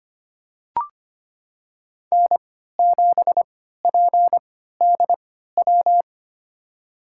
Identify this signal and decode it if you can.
Morse code, heard as a keyed tone, about 25 wpm, spaced as for 14 wpm, 700 Hz. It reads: N7PDW
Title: USA Call Signs - 3x Character Spacing - 25wpm